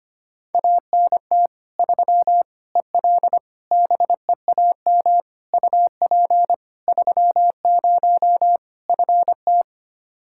Morse code, heard as a keyed tone, about 25 wpm, 700 Hz. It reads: ANT 3 EL BEAM UP 30 FT
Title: Practice QSO - Exchange 2 - Info - 25wpm